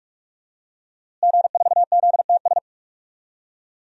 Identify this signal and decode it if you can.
Morse code, heard as a keyed tone, about 47 wpm, 700 Hz. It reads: G4ZTS
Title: DXCC-100 Call Signs - 5 Characters - Repeated 3x - 45wpm